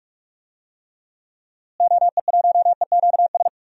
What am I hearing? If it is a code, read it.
Morse code, heard as a keyed tone, about 46 wpm, 700 Hz. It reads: OE1EQS
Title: DXCC-100 Call Signs - 6 Characters - Repeated 3x - 45wpm